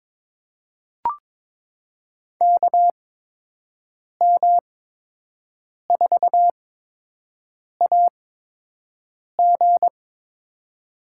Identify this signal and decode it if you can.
Morse code, heard as a keyed tone, about 23 wpm, spaced as for 6 wpm, 700 Hz. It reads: KM4AG
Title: USA Call Signs - 5 Characters - 8x Character Spacing - 22wpm